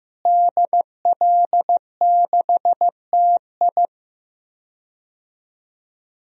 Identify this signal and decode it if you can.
Morse code, heard as a keyed tone, about 16 wpm, 700 Hz. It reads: DL6TI
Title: DXCC-100 Call Signs - 5 Characters - Repeated 2x - 15wpm